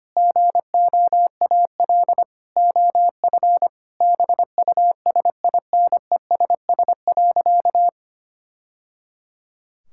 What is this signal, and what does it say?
Morse code, heard as a keyed tone, about 25 wpm, 700 Hz. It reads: GOAL OF BUSINESS.
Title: Sentences from Top 1000 Words - Review Part 5 - 25wpm